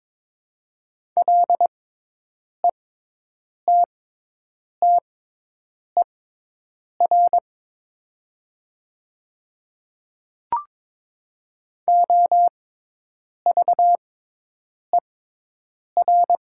Morse code, heard as a keyed tone, about 21 wpm, spaced as for 8 wpm, 700 Hz. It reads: LETTER OVER
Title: 6x Character Spacing - Top 300 Words - 22wpm